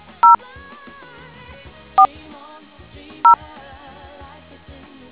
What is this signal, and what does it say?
DTMF, heard as telephone touch tones, tones of about 120 ms. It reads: 0 4 0